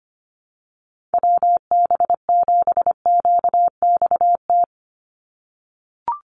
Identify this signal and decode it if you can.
Morse code, heard as a keyed tone, about 25 wpm, 700 Hz. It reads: WB7QXT
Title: USA Call Signs 25wpm